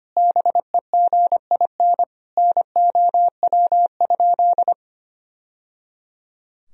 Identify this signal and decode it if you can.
Morse code, heard as a keyed tone, about 25 wpm, 700 Hz. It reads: BEGIN NOW?
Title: Sentences from Top 1000 Words - Review Part 1 - 25wpm